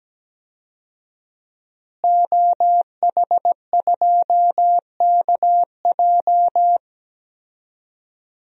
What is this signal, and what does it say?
Morse code, heard as a keyed tone, about 17 wpm, 700 Hz. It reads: OH2KJ
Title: DXCC-100 Call Signs - 5 Characters - Repeated 2x - 17wpm